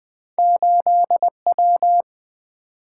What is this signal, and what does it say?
Morse code, heard as a keyed tone, about 28 wpm, spaced as for 24 wpm, 700 Hz. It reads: 8W
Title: Speed-Racing - Two Letter-Number Combinations - 20wpm